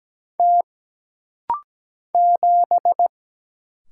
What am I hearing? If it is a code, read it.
Morse code, heard as a keyed tone, about 17 wpm, 700 Hz. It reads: T 7
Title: Single Letter-Number - Rapid-Fire 17wpm